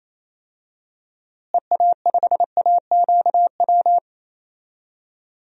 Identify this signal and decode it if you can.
Morse code, heard as a keyed tone, about 28 wpm, 700 Hz. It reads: EA5AQW
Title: DXCC-100 Call Signs - 6 Characters - Repeated 3x - 28wpm